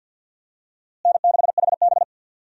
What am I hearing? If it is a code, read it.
Morse code, heard as a keyed tone, about 53 wpm, 700 Hz. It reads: N6HB
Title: USA Call Signs - 4 Characters - Repeated 2x - 50wpm